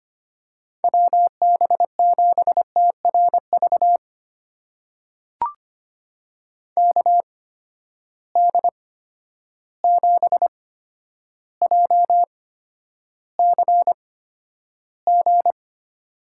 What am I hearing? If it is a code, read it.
Morse code, heard as a keyed tone, about 25 wpm, spaced as for 11 wpm, 700 Hz. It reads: WB7TRV KD7JCG
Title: USA Call Signs - 6 Characters - 8x Character Spacing - 25wpm